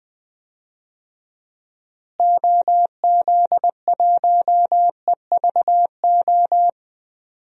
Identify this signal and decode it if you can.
Morse code, heard as a keyed tone, about 21 wpm, 700 Hz. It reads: OZ1EVO
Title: DXCC-100 Call Signs - 6 Characters - Repeated 2x - 20wpm